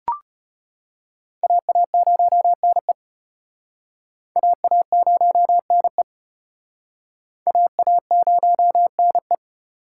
Morse code, heard as a keyed tone, about 33 wpm, 700 Hz. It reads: AA0NE AA0NE AA0NE